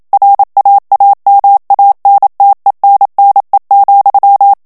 Morse code, heard as a keyed tone, about 27 wpm, 800 Hz. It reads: RAAMANTENNE,
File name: HOLLANDAIS texte clair 1 vitesse 26 mots minute